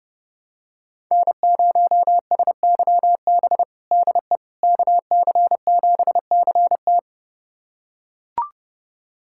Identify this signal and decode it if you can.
Morse code, heard as a keyed tone, about 30 wpm, 700 Hz. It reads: N0SYB DE KC7CT
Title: USA Callsign DE Callsign - 5 Characters - 30wpm